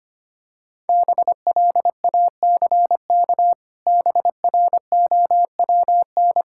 Morse code, heard as a keyed tone, about 25 wpm, 700 Hz. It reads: BLACK BROWN
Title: Pangram Challenge v2 - 25wpm